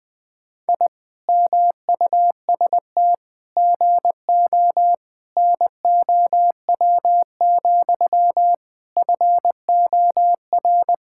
Morse code, heard as a keyed tone, about 20 wpm, 700 Hz. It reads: I MUST GO NOW, FOR
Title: Sentences from Top 1000 Words - Review Part 5 - 20wpm